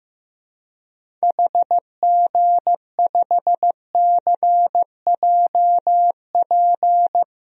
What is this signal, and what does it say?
Morse code, heard as a keyed tone, about 15 wpm, 700 Hz. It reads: HG5CJP
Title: DXCC-100 Call Signs - 6 Characters - 15wpm